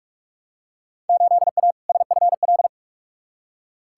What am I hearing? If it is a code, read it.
Morse code, heard as a keyed tone, about 45 wpm, 700 Hz. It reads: 8A SFL
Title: Field Day Exchanges Encore - 45wpm